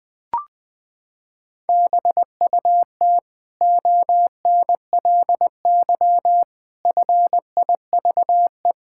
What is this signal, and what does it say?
Morse code, heard as a keyed tone, about 21 wpm, 700 Hz. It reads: BUT ONLY FIVE
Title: Sentences from Top 1000 Words - Review Part 6 - 20wpm